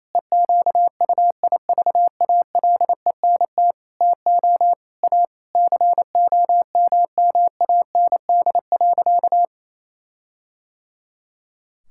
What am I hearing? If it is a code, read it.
Morse code, heard as a keyed tone, about 28 wpm, 700 Hz. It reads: EQUIVALENT TO A COMMAND.
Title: Sentences from Top 1600 Words 28wpm